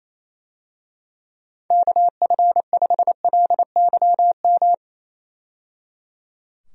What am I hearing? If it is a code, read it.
Morse code, heard as a keyed tone, about 27 wpm, 700 Hz. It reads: KF5LYM
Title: USA Call Signs - 6 Characters - 28wpm